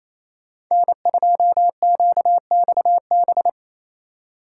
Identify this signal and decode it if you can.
Morse code, heard as a keyed tone, about 28 wpm, 700 Hz. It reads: N2QXB
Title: Speed Racing - USA 1x3 Call Signs 28wpm